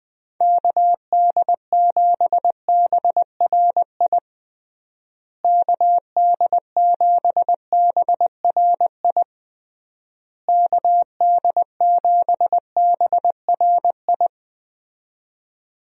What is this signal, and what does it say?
Morse code, heard as a keyed tone, about 21 wpm, 700 Hz. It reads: KD7BRI KD7BRI KD7BRI
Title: USA Call Signs - 6 Characters - Repeated 3x - 20wpm